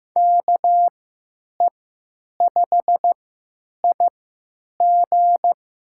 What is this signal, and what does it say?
Morse code, heard as a keyed tone, about 15 wpm, spaced as for 9 wpm, 700 Hz. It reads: KE5IG
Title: USA Call Signs - 6 Characters - 3x Character Spacing - 15wpm